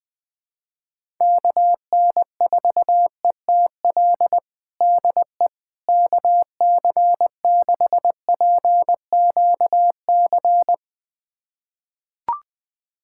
Morse code, heard as a keyed tone, about 20 wpm, 700 Hz. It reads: KN4ETL DE KC6PQC
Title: USA Callsign DE Callsign 20wpm